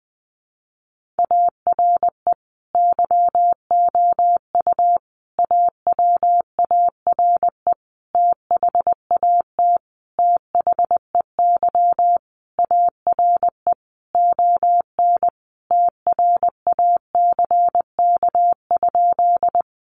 Morse code, heard as a keyed tone, about 20 wpm, 700 Hz. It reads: ARE YOU AWARE THAT THEY ARE ON TRACK?
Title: Sentences from Top 1200 Words 20wpm